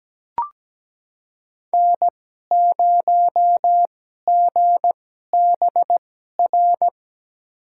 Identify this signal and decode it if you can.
Morse code, heard as a keyed tone, about 17 wpm, spaced as for 12 wpm, 700 Hz. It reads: N0GBR
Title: USA Call Signs - 5 Characters - 2x Character Spacing - 17wpm